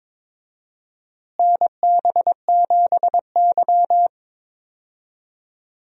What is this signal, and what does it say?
Morse code, heard as a keyed tone, about 21 wpm, 700 Hz. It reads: NB7Y
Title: USA Call Signs - 4 Characters - Repeated 2x - 22wpm